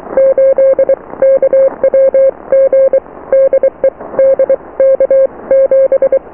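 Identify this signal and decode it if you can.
Morse code, heard as a keyed tone, about 23 wpm, spaced as for 20 wpm, 540 Hz. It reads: 8KWG DE DK7
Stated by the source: Morse QSO on 14 MHz with a German station, received at RST 579.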